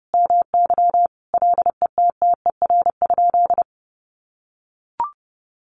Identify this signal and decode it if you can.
Morse code, heard as a keyed tone, about 30 wpm, 700 Hz. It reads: MY LETTER?
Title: Sentences from Top 600 Words 30wpm